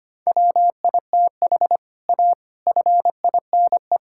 Morse code, heard as a keyed tone, about 25 wpm, 700 Hz. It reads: WITH A FINE
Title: Sentences from Top 1000 Words - Review Part 5 - 25wpm